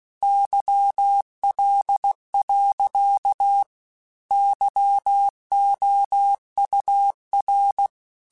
Morse code, heard as a keyed tone, about 16 wpm, 800 Hz. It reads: YL. YOUR
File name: QSOs / QSOs-mp3s / WA7QOX 2@15